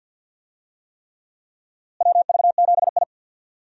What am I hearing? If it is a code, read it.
Morse code, heard as a keyed tone, about 46 wpm, 700 Hz. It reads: WV7I
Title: USA Call Signs - 4 Characters - Repeated 3x - 50wpm